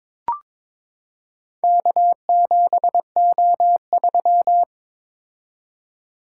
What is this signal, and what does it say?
Morse code, heard as a keyed tone, about 22 wpm, 700 Hz. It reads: K7O3
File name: practice / 22wpm / Four Letter-Number Combinations - 22wpm